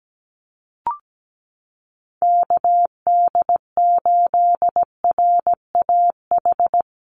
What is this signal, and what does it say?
Morse code, heard as a keyed tone, about 17 wpm, 700 Hz. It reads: KD8RAH
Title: USA Call Signs 17wpm